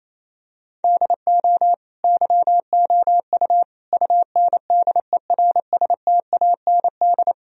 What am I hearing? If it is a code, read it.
Morse code, heard as a keyed tone, about 28 wpm, 700 Hz. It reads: DO YOU UNDERSTAND
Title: Sentences from Top 1000 Words - Review Part 4 - 28wpm